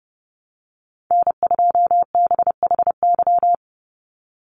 Morse code, heard as a keyed tone, about 30 wpm, 700 Hz. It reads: N2BHY
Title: USA Call Signs 30wpm